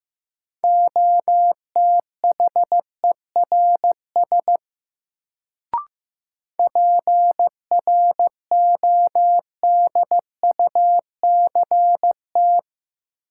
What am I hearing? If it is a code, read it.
Morse code, heard as a keyed tone, about 15 wpm, 700 Hz. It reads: OTHERS PRODUCT
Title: Sentences from Top 1000 Words - Review Part 2 - New Word Drill - 15wpm